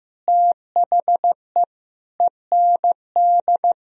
Morse code, heard as a keyed tone, about 15 wpm, 700 Hz. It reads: THE END
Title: Sentences from Top 300 Words and Abbreviations 15wpm